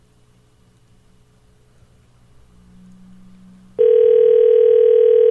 Ringback tone (North America), 440 + 480 Hz.